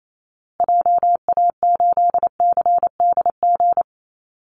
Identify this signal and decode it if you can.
Morse code, heard as a keyed tone, about 28 wpm, 700 Hz. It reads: JA8CDG
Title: Call Signs Repeated 2x - 28wpm